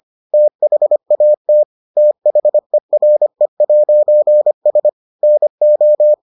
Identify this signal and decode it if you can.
Morse code, heard as a keyed tone, about 25 wpm, 600 Hz. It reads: THAT THERE'S NO